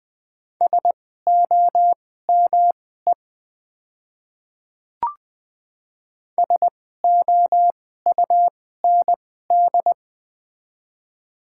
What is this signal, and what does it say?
Morse code, heard as a keyed tone, about 20 wpm, spaced as for 14 wpm, 700 Hz. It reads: SOME SOUND
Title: 2x Character Spacing - Top 100 Words - 20wpm